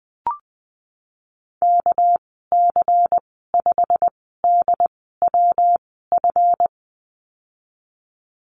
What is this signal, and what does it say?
Morse code, heard as a keyed tone, about 20 wpm, spaced as for 15 wpm, 700 Hz. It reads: KC5DWF
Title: 200 Lesson 32g - Letter k - Call Signs 20wpm